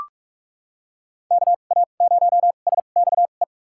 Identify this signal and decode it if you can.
Morse code, heard as a keyed tone, about 45 wpm, spaced as for 32 wpm, 700 Hz. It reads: KA0SXE
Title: USA Call Signs - 6 Characters - 2x Character Spacing - 45wpm